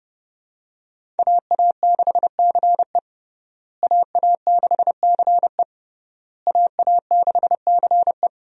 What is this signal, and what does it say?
Morse code, heard as a keyed tone, about 32 wpm, 700 Hz. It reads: AA6CE AA6CE AA6CE
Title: USA Call Signs - 5 Characters - Repeated 3x - 30wpm